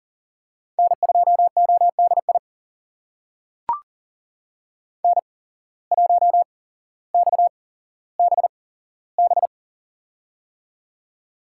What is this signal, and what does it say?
Morse code, heard as a keyed tone, about 40 wpm, spaced as for 17 wpm, 700 Hz. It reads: N2ODI N1XBB
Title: USA Call Signs - 5 Characters - 8x Character Spacing - 40wpm